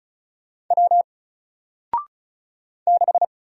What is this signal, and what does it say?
Morse code, heard as a keyed tone, about 36 wpm, 700 Hz. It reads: W 6
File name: Single-Letter-Number-35wpm